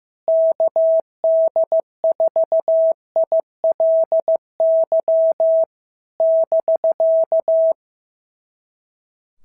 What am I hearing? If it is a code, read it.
Morse code, heard as a keyed tone, about 15 wpm, 650 Hz. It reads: KD4ILY <BK>
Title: SOTA QSO Practice 15wpm